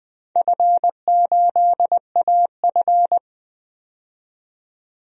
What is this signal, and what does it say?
Morse code, heard as a keyed tone, about 20 wpm, 700 Hz. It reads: F8AF
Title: DXCC-100 Call Signs - 4 Characters - 20wpm